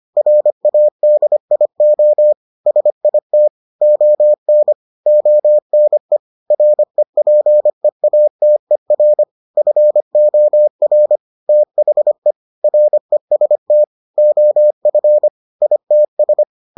Morse code, heard as a keyed tone, about 25 wpm, 600 Hz. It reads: RADIO SIT ON ONE REPEATER FOR THE REST OF ITS